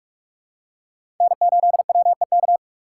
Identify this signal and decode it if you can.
Morse code, heard as a keyed tone, about 44 wpm, 700 Hz. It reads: N8WEK
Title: USA Call Signs - 3x Character Spacing - 45wpm